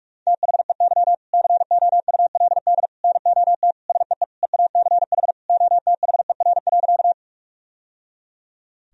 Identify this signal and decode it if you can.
Morse code, heard as a keyed tone, about 46 wpm, 700 Hz. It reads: THEY COULD NOT SEE EACH OTHER.
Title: Sentences from Top 1000 Words - Review Part 1 - 45wpm